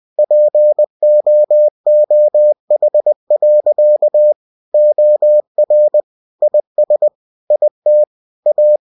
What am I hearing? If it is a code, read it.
Morse code, heard as a keyed tone, about 20 wpm, 600 Hz. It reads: POOH. OR IS IT A
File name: cw-books / winnie-the-pooh / winnie-20wpm-page-0069